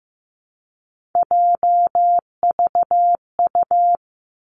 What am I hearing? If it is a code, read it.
Morse code, heard as a keyed tone, about 15 wpm, 700 Hz. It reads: JVU
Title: Three Challenging Letter-Number Combinations 15wpm